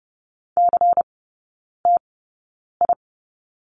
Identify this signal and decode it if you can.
Morse code, heard as a keyed tone, about 30 wpm, spaced as for 9 wpm, 690 Hz. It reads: CTI
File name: Lesson 16e - Letter c - Three Characters 30wpm